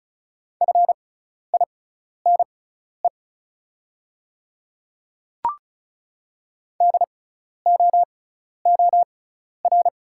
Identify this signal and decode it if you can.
Morse code, heard as a keyed tone, about 34 wpm, spaced as for 12 wpm, 690 Hz. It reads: FINE DOOR
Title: 6x Character Spacing - Top 500 Words - 35wpm